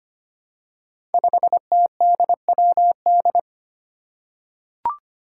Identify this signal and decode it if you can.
Morse code, heard as a keyed tone, about 25 wpm, 700 Hz. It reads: 5TDWD
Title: Five Challenging Letter-Number Combinations 25wpm